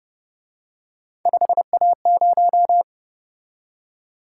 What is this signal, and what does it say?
Morse code, heard as a keyed tone, about 29 wpm, 700 Hz. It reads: 5A0
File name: Speed-Racing - Three Letter-Number Combinations - 25wpm